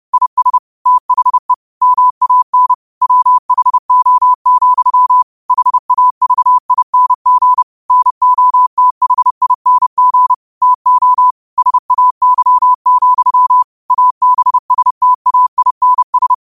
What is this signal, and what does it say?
Morse code, heard as a keyed tone, about 30 wpm, 1000 Hz. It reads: IS THE MAN WHO, HAVING NOTHING TO SAY, ABSTAINS
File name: qotdfiles / 30wpm